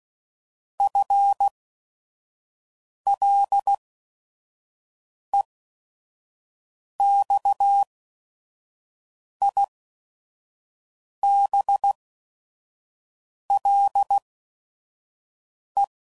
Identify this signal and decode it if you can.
Morse code, heard as a keyed tone, about 16 wpm, spaced as for 5 wpm, 800 Hz. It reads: FLEXIBLE